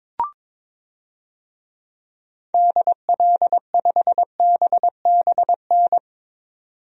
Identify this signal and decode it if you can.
Morse code, heard as a keyed tone, about 22 wpm, 700 Hz. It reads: DL5BBN
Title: DXCC-100 Call Signs - 6 Characters - 22wpm